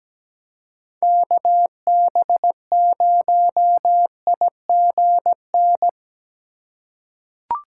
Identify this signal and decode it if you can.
Morse code, heard as a keyed tone, about 17 wpm, 700 Hz. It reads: KB0IGN
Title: USA Call Signs - 6 Characters - 2x Character Spacing - 17wpm